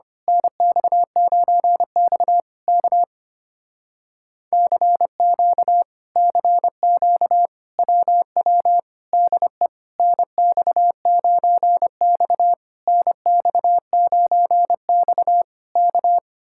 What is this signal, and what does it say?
Morse code, heard as a keyed tone, about 24 wpm, 700 Hz. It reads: NX9X K CQ CQ WW DE NX9X NX9X K
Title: Speed-Racing Common QSO Phrases Encore - 25wpm